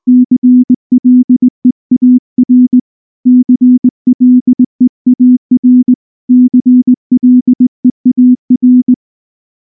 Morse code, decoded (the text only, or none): CLEAR CLEAR CLEAR